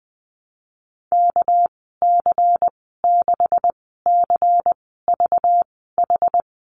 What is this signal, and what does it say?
Morse code, heard as a keyed tone, about 19 wpm, spaced as for 14 wpm, 700 Hz. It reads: KC6CVH
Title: 207 Lesson 33g - Letter j - Call Signs 20wpm